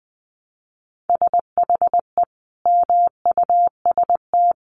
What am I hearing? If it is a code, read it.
Morse code, heard as a keyed tone, about 20 wpm, 700 Hz. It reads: SHE MUST
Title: Sentences from Top 1100 Words 20wpm